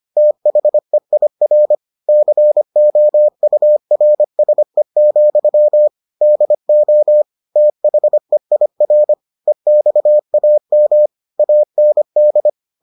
Morse code, heard as a keyed tone, about 25 wpm, 600 Hz. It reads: THEIR COURSE, DO THEIR EXAM AND